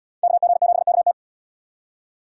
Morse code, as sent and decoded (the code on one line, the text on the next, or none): -.- --. --... -.- -
KG7KT